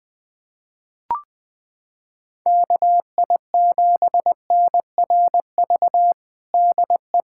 Morse code, heard as a keyed tone, about 20 wpm, 700 Hz. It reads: KI7NRV DE
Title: USA Callsign DE Callsign 20wpm